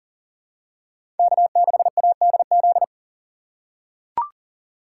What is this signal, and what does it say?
Morse code, heard as a keyed tone, about 40 wpm, 700 Hz. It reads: K6ADZ
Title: USA Call Signs - 5 Characters - 3x Character Spacing - 40wpm